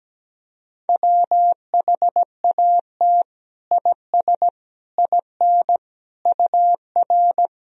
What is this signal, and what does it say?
Morse code, heard as a keyed tone, about 17 wpm, 700 Hz. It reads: WHAT IS IN UR
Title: Sentences from Top 200 Words and Abbreviations 17wpm